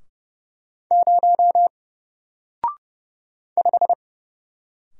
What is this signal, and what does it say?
Morse code, heard as a keyed tone, about 29 wpm, 700 Hz. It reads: 0 5
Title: Single Letter-Number Alt 30wpm